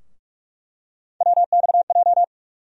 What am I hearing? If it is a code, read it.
Morse code, heard as a keyed tone, about 46 wpm, 700 Hz. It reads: WXJ
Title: USA Call Sign Postfixes - 3 Letters 45wpm